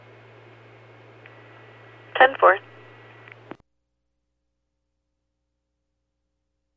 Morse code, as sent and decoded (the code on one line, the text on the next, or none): none